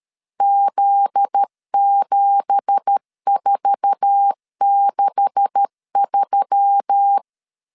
Morse code, heard as a keyed tone, about 13 wpm, 800 Hz. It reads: Z7463
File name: LEZIONI / 9 2867 12wpm 2322